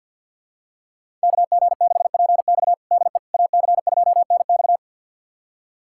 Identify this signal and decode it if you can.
Morse code, heard as a keyed tone, about 52 wpm, 700 Hz. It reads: KG6PX DE AC2NX
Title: USA Callsign DE Callsign - 5 Characters - 50wpm